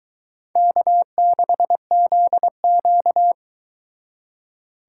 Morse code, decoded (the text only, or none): K6ZQ